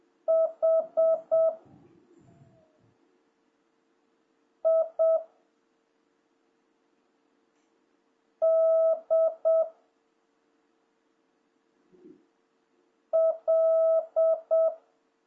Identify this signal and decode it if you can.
Morse code, heard as a keyed tone, about 7 wpm, 1300 Hz. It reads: H I D L
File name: jh8cbh / 2026cw